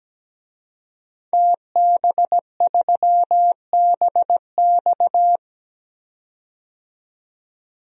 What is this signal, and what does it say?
Morse code, heard as a keyed tone, about 17 wpm, 700 Hz. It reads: TB3BX
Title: DXCC-100 Call Signs - 5 Characters - 17wpm